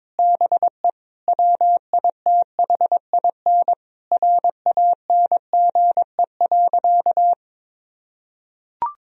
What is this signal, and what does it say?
Morse code, heard as a keyed tone, about 22 wpm, 700 Hz. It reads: BE WITHIN RANGE.
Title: Sentences from Top 1000 Words - Review Part 4 - 22wpm